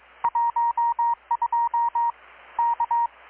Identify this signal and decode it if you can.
Morse code, heard as a keyed tone, about 22 wpm, 940 Hz, with radio noise. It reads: 12 K
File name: blog / swl / 6-abril-2019-12695-USB-KPS